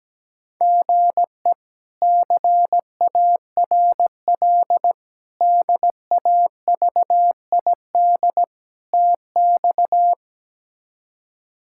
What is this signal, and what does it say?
Morse code, heard as a keyed tone, about 17 wpm, 700 Hz. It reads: GE CARL DAVID TX
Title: K1USN SST QSO Practice - 17wpm